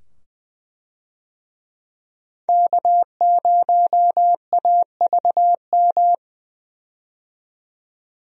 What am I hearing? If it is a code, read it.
Morse code, heard as a keyed tone, about 20 wpm, 700 Hz. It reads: K0AVM
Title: USA Call Signs - 5 Characters - 20wpm